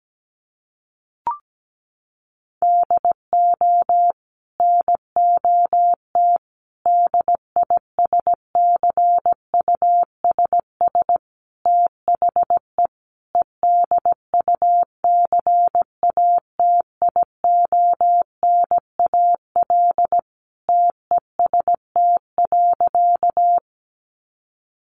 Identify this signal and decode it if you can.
Morse code, heard as a keyed tone, about 17 wpm, 700 Hz. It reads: DO NOT DISCUSS THE EDUCATIONAL TEST.
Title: Sentences from Top 1200 Words 17wpm